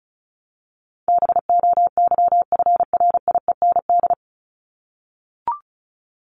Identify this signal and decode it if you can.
Morse code, heard as a keyed tone, about 34 wpm, 700 Hz. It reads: BOYFRIEND
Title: Top 501-750 Words 35wpm